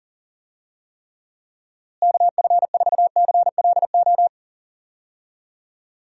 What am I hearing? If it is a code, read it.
Morse code, heard as a keyed tone, about 40 wpm, 700 Hz. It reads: KF4CLO
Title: USA Call Signs - 6 Characters - Repeated 3x - 40wpm